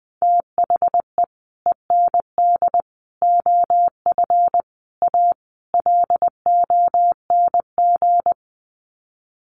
Morse code, heard as a keyed tone, about 19 wpm, 700 Hz. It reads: THE END OF A LONG
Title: Sets of 5 Words Encore2 20wpm